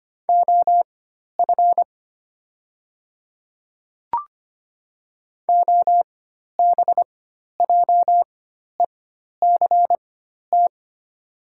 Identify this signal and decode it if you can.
Morse code, heard as a keyed tone, about 25 wpm, spaced as for 12 wpm, 700 Hz. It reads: OF OBJECT
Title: 4x Character Spacing - Top 500 Words - 25wpm